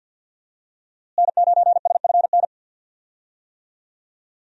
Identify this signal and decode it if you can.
Morse code, heard as a keyed tone, about 52 wpm, 700 Hz. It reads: N9SFN
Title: USA Call Signs - 5 Characters - Repeated 3x - 50wpm